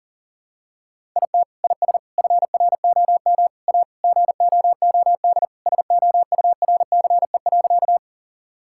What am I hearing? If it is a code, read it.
Morse code, heard as a keyed tone, about 40 wpm, 700 Hz. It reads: IT IS FROM A GOOD SOURCE.
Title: Sentences from Top 1000 Words - Review Part 5 - 40wpm